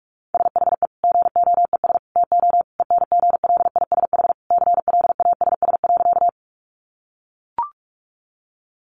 Morse code, heard as a keyed tone, about 44 wpm, 700 Hz. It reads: SHE GOES TO ENGLISH CLASS.
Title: Additional Easy Phrases 45wpm